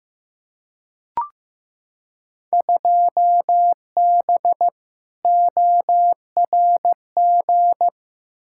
Morse code, heard as a keyed tone, about 15 wpm, 700 Hz. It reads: 2B ORG
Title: Field Day Exchanges Encore - 15wpm